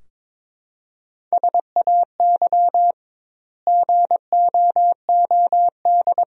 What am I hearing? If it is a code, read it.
Morse code, heard as a keyed tone, about 22 wpm, 700 Hz. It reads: SAY GOOD
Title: 2x Word-Spacing - Sentences from Top 300 Words - 22wpm